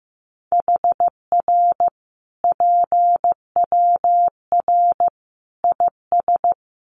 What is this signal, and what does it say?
Morse code, heard as a keyed tone, about 15 wpm, 700 Hz. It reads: HR PWR IS
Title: Common QSO Phrases 15wpm